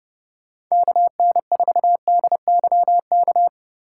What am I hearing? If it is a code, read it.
Morse code, heard as a keyed tone, about 28 wpm, 700 Hz. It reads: KN4DYK